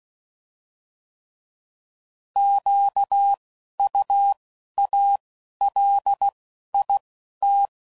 Morse code, heard as a keyed tone, about 16 wpm, spaced as for 11 wpm, 800 Hz. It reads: QUALIT